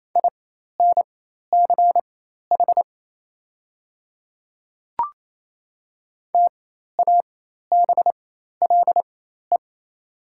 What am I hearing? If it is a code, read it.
Morse code, heard as a keyed tone, about 27 wpm, spaced as for 13 wpm, 700 Hz. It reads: INCH TABLE
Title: 4x Character Spacing - Top 500 Words - 28wpm